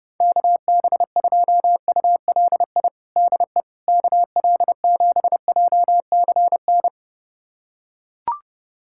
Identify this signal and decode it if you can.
Morse code, heard as a keyed tone, about 32 wpm, 700 Hz. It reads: KB2ULI DE KL7JCN
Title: USA Callsign DE Callsign 30wpm